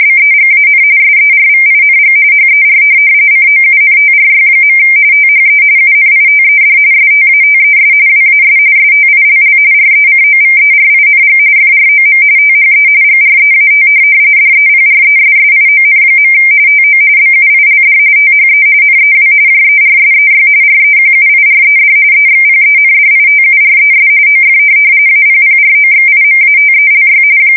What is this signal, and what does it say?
RTTY, about 45.45 baud. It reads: BEGAN SERVING HER COUNTRY. THE UNITED STATES WAS STILL RECOVERING FROM THE GREAT DEPRESSION. TO BUILD THE NEW JERSEY REQUIRED EXPENDITURES OF APPROXIMATELY
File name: fsk170 / USS-New-Jersey